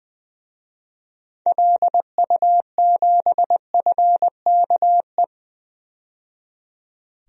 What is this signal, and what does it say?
Morse code, heard as a keyed tone, about 21 wpm, 700 Hz. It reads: LU7FKE
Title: DXCC-100 Call Signs - 6 Characters - 20wpm